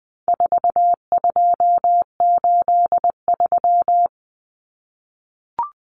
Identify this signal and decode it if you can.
Morse code, heard as a keyed tone, about 20 wpm, 700 Hz. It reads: 4283
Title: Common Cut Numbers - 4-Digit 20wpm